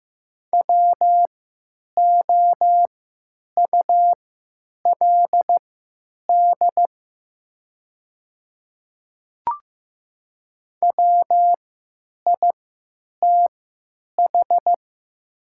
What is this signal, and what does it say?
Morse code, heard as a keyed tone, about 15 wpm, spaced as for 8 wpm, 700 Hz. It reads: WOULD WITH
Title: 3x Character Spacing - Top 200 Words - 15wpm